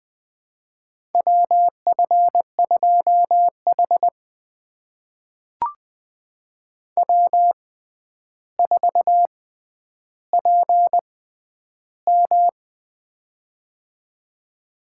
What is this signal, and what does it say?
Morse code, heard as a keyed tone, about 20 wpm, spaced as for 10 wpm, 700 Hz. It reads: WF2H W4PM
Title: USA Call Signs - 4 Characters - 6x Character Spacing - 20wpm